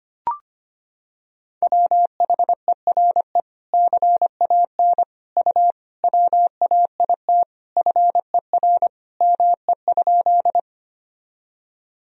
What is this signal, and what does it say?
Morse code, heard as a keyed tone, about 25 wpm, 700 Hz. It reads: WHERE CAN U WAIT FER ME?
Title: Sentences from Top 500 Words and Abbreviations 25wpm